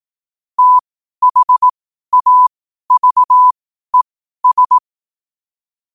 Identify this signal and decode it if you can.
Morse code, heard as a keyed tone, about 18 wpm, spaced as for 12 wpm, 1000 Hz. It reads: THAVES
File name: qotdfiles / 12wpm